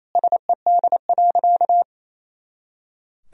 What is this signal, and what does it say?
Morse code, heard as a keyed tone, about 27 wpm, 700 Hz. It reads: SED.
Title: Sentences from Top 200 Words and Abbreviations 28wpm